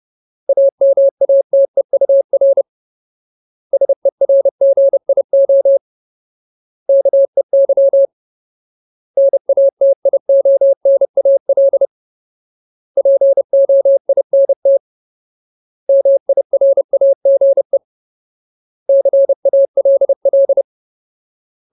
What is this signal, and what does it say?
Morse code, heard as a keyed tone, about 30 wpm, 550 Hz. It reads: AMATEUR SERGIO KEY NATIONAL POINT MIRAGE CALL
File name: Ultimate Word List / 30WPM Ultimate